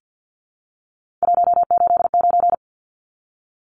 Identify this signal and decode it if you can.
Morse code, heard as a keyed tone, about 51 wpm, 700 Hz. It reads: 189
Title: Numbers - 3-Digit 50wpm